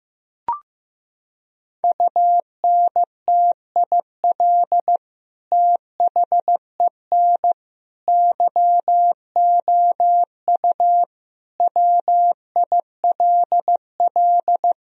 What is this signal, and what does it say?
Morse code, heard as a keyed tone, about 15 wpm, 700 Hz. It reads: UNTIL THEN YOU WILL
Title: Sentences from Top 1000 Words - Review Part 1 - 15wpm